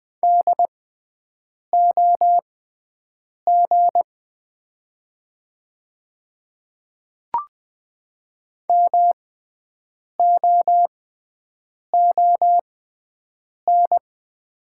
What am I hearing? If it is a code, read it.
Morse code, heard as a keyed tone, about 20 wpm, spaced as for 7 wpm, 700 Hz. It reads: DOG MOON
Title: 6x Character Spacing - Top 500 Words - 20wpm